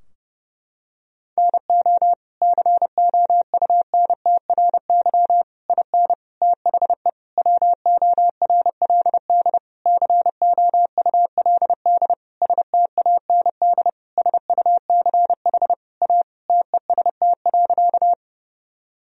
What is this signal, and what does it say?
Morse code, heard as a keyed tone, about 30 wpm, 700 Hz. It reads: NO COUNTRY IN THE WORLD COULD STAND SUCH A TEST.
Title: Sentences from Top 1000 Words - Review Part 3 - 30wpm